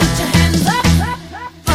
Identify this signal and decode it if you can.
There is no code.